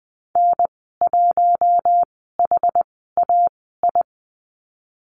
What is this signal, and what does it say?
Morse code, heard as a keyed tone, about 20 wpm, spaced as for 15 wpm, 700 Hz. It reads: N1HAI
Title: 062 Lesson 11g - Letter h - Call Signs 20wpm